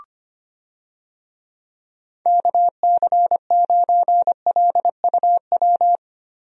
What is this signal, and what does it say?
Morse code, heard as a keyed tone, about 25 wpm, 700 Hz. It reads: KC9LUW
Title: USA Call Signs - 6 Characters - 25wpm